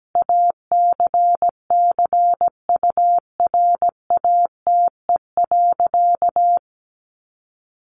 Morse code, heard as a keyed tone, about 17 wpm, 700 Hz. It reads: ACCURATE.
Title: Sentences from Top 1200 Words 17wpm